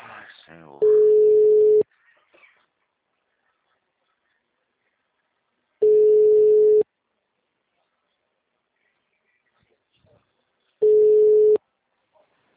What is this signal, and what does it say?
Ringback tone (Europe, 425 Hz), with speech.